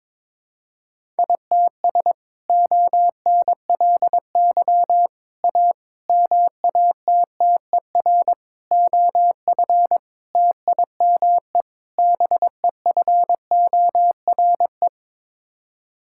Morse code, heard as a keyed tone, about 22 wpm, 700 Hz. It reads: ITS ONLY A MATTER OF TIME BEFORE
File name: practice / 22wpm / Sets of 7 Words - 22wpm